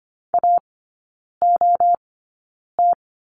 Morse code, heard as a keyed tone, about 25 wpm, spaced as for 9 wpm, 700 Hz. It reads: AOT
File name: Lesson 14e - Number 2 - Three Characters 25wpm